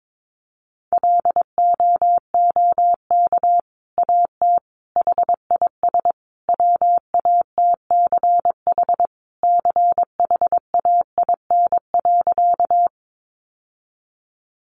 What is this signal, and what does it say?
Morse code, heard as a keyed tone, about 22 wpm, 700 Hz. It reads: LOOK AT HIS WATCH CHAIN.
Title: Sentences from Top 1500 Words 22wpm